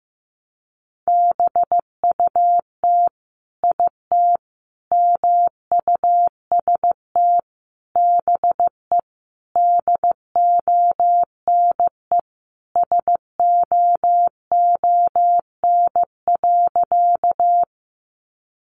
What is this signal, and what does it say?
Morse code, heard as a keyed tone, about 15 wpm, 700 Hz. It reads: BUT IT MUST BE DONE SOON.
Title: Sentences from Top 500 Words 15wpm